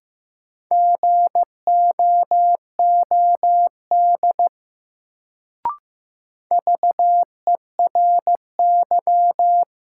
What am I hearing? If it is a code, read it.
Morse code, heard as a keyed tone, about 15 wpm, 700 Hz. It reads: GOOD VERY
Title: Sentences from Top 200 Words - New Word Drill - 15wpm